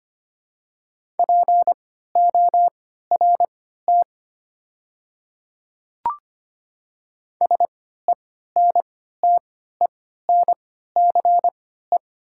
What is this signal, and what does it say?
Morse code, heard as a keyed tone, about 25 wpm, spaced as for 14 wpm, 700 Hz. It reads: PORT SENTENCE
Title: 3x Character Spacing - Top 300 Words - 25wpm